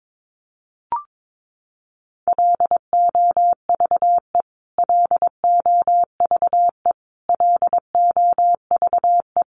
Morse code, heard as a keyed tone, about 22 wpm, 700 Hz. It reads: LOVE LOVE LOVE
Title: Top 500 Words Repeated 22wpm